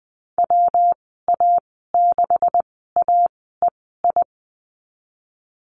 Morse code, heard as a keyed tone, about 20 wpm, spaced as for 15 wpm, 700 Hz. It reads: WA6AEI
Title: 173 Lesson 28g - Number 9 - Call Signs 20wpm